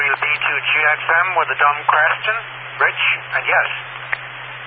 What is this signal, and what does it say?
Emergency Alert System (SAME) tones, heard as harsh buzzing bursts, with speech. It is heard from 0.8 to 3.7 s.